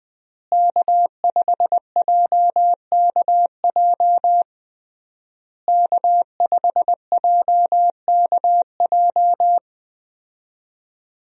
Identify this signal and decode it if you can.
Morse code, heard as a keyed tone, about 20 wpm, 700 Hz. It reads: K5JKJ K5JKJ
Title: USA Call Signs - 5 Characters - Repeated 3x - 20wpm